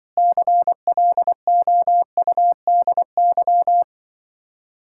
Morse code, heard as a keyed tone, about 24 wpm, 700 Hz. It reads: CLOUDY
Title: Speed-Racing Common QSO Phrases - 20wpm